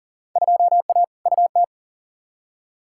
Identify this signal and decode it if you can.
Morse code, heard as a keyed tone, about 40 wpm, 700 Hz. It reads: 2A UT
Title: Field Day Exchanges - 40wpm